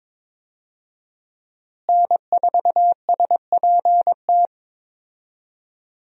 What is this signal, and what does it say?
Morse code, heard as a keyed tone, about 23 wpm, 700 Hz. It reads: N4SPT
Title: USA Call Signs - 5 Characters - Repeated 2x - 22wpm